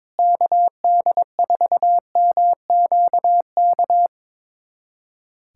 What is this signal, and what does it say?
Morse code, heard as a keyed tone, about 22 wpm, 700 Hz. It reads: KD4MQK
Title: USA Callsign DE Callsign - 6 Characters - 22wpm